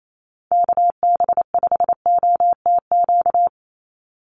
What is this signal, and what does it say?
Morse code, heard as a keyed tone, about 28 wpm, 700 Hz. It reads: KB5OTQ
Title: USA Call Signs 28wpm